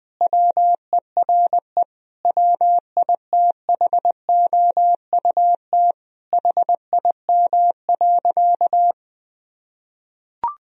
Morse code, heard as a keyed tone, about 21 wpm, 700 Hz. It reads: WERE WITHOUT HIM.
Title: Sentences from Top 1000 Words - Review Part 3 - 20wpm